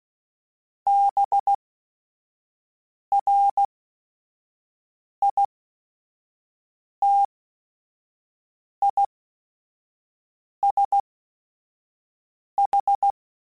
Morse code, heard as a keyed tone, about 16 wpm, spaced as for 5 wpm, 800 Hz. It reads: BRITISH
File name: Suffixes / General / ISHs